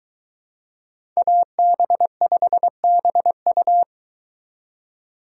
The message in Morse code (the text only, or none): AB5BU